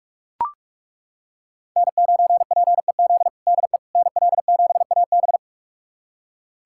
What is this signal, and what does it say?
Morse code, heard as a keyed tone, about 41 wpm, 700 Hz. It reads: N9PEZ DE NL7AB
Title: USA Callsign DE Callsign - 5 Characters - 45wpm